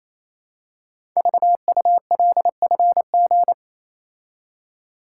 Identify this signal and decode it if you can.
Morse code, heard as a keyed tone, about 27 wpm, 690 Hz. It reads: VULFG
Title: Code Groups - 5 Letters - 28wpm